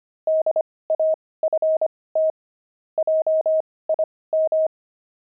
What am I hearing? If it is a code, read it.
Morse code, heard as a keyed tone, about 25 wpm, spaced as for 18 wpm, 630 Hz. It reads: DAFT JIM
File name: pangram-fs-18-cs-25-zebras